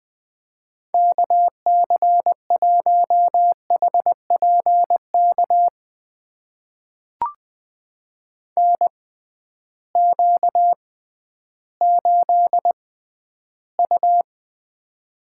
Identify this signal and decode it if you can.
Morse code, heard as a keyed tone, about 20 wpm, spaced as for 12 wpm, 700 Hz. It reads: KC1HPK NQ8U
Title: USA Call Signs - 6x Character Spacing - 20wpm